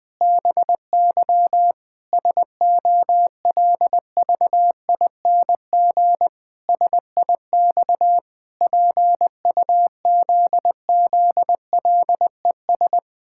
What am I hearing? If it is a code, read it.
Morse code, heard as a keyed tone, about 21 wpm, 700 Hz. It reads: BY SOLVING SIX PUZZLES
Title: Pangram Challenge v2 - 20wpm